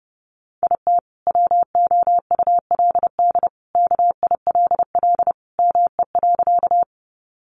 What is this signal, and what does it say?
Morse code, heard as a keyed tone, about 30 wpm, 700 Hz. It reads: IT WOULD KILL ME.
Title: Sentences from Top 1400 Words 30wpm